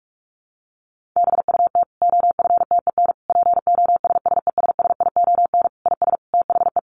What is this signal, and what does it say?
Morse code, heard as a keyed tone, about 44 wpm, 700 Hz. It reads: BUT OFTEN POSSESSION IS THE
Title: Sentences from Top 1400 Words 45wpm